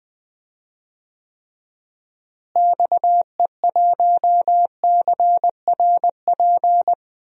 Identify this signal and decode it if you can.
Morse code, heard as a keyed tone, about 20 wpm, 700 Hz. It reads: XE1CRP
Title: DXCC-100 Call Signs - 6 Characters - Repeated 3x - 20wpm